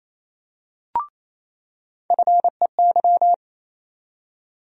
Morse code, heard as a keyed tone, about 28 wpm, 700 Hz. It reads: FEY
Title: USA Call Sign Postfixes - 3 Letters 28wpm